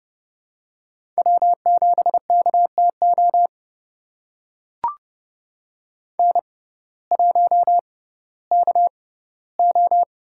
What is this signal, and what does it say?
Morse code, heard as a keyed tone, about 30 wpm, spaced as for 16 wpm, 700 Hz. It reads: W7KTO N1KO
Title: USA Call Signs - 5 Characters - 6x Character Spacing - 30wpm